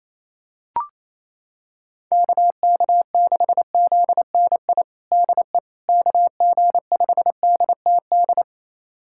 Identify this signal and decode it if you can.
Morse code, heard as a keyed tone, about 28 wpm, 700 Hz. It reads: KK6ZNI DE KG5DTD
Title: USA Callsign DE Callsign - 6 Characters - 28wpm